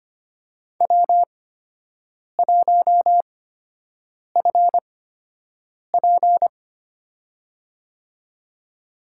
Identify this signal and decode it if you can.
Morse code, heard as a keyed tone, about 25 wpm, spaced as for 7 wpm, 700 Hz. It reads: W1FP
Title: USA Call Signs - 4 Characters - 8x Character Spacing - 25wpm